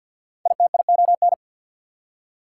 Morse code, heard as a keyed tone, about 53 wpm, 690 Hz. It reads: ITION
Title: Prefixes and Suffixes 50wpm